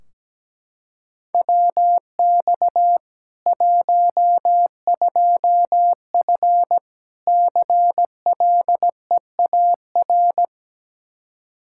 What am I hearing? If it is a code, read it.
Morse code, heard as a keyed tone, about 17 wpm, 700 Hz. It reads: WX 12F CLEAR
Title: Practice QSO - Exchange 2 - Info - 17wpm